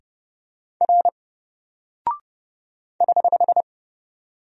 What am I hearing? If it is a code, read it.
Morse code, heard as a keyed tone, about 28 wpm, 700 Hz. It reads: R <HH>
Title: Single Character (Letter:Number:Punctuation:Prosign) 30wpm